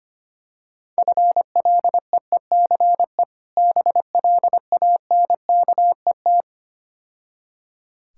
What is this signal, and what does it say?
Morse code, heard as a keyed tone, about 25 wpm, 700 Hz. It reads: FLEECE BLANKET
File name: practice / 25wpm / Difficult Words - Speed-Racing with Hints - 25wpm